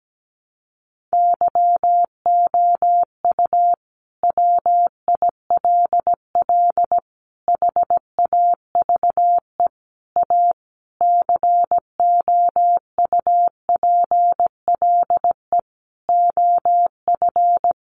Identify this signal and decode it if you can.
Morse code, heard as a keyed tone, about 17 wpm, 700 Hz. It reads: YOU WILL HAVE A COUPLE OF
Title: Sentences from Top 1100 Words 17wpm